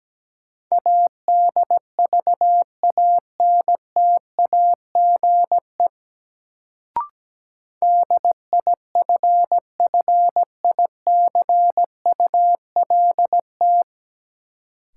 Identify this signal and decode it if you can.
Morse code, heard as a keyed tone, about 17 wpm, 700 Hz. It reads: ADVANTAGE DIFFICULT
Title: Sets of 4 Words - New Word Drill - 17wpm